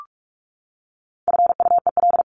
Morse code, heard as a keyed tone, about 44 wpm, 700 Hz. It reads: FUEL
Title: Top 1751-2000 Words 45wpm